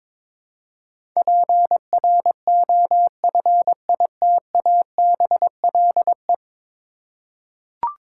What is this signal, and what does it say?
Morse code, heard as a keyed tone, about 21 wpm, 700 Hz. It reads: PROFITABLE
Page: morse-code-ninja-practice-22wpm